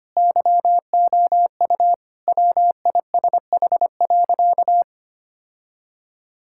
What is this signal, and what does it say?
Morse code, heard as a keyed tone, about 25 wpm, 700 Hz. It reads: YOU WISH.
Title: Sentences from Top 1600 Words 25wpm